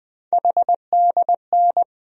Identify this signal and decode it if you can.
Morse code, heard as a keyed tone, about 20 wpm, 700 Hz. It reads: HDN
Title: USA Call Sign Postfixes - 3 Letters 20wpm